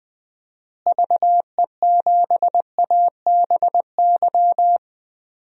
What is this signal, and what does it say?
Morse code, heard as a keyed tone, about 20 wpm, 700 Hz. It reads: VE7ABY
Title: DXCC-100 Call Signs - 6 Characters - Repeated 2x - 20wpm